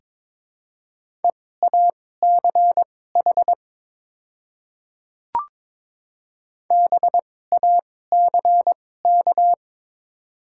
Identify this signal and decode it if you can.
Morse code, heard as a keyed tone, about 22 wpm, spaced as for 16 wpm, 700 Hz. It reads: EACH BACK
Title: 2x Character Spacing - Top 200 Words - 22wpm